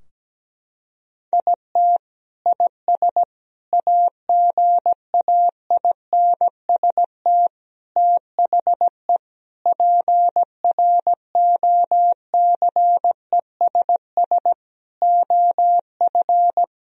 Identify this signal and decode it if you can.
Morse code, heard as a keyed tone, about 17 wpm, 700 Hz. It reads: IT IS AGAINST THE PROCESS OF
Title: Sentences from Top 1000 Words - Review Part 3 - 17wpm